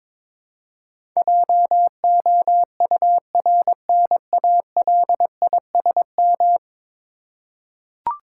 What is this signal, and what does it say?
Morse code, heard as a keyed tone, about 22 wpm, 700 Hz. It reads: JOURNALISM
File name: practice / 22wpm / 10 Letter Words - First 3 Letters Given - 22wpm